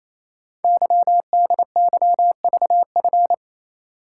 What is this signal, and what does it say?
Morse code, heard as a keyed tone, about 28 wpm, 700 Hz. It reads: YDYVF
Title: Code Groups - 5 Letters - 28wpm